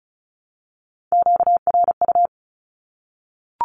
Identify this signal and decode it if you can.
Morse code, heard as a keyed tone, about 34 wpm, 700 Hz. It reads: QRU
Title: Q-Codes 35wpm